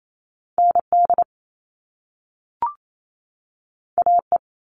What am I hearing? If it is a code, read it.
Morse code, heard as a keyed tone, about 28 wpm, 700 Hz. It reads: ND AE
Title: Top Two Letter Combinations in Words 28wpm